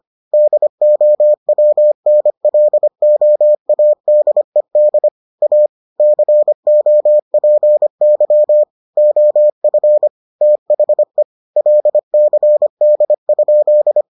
Morse code, heard as a keyed tone, about 25 wpm, 600 Hz. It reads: DOWNLOADED A COPY OF THE LCD?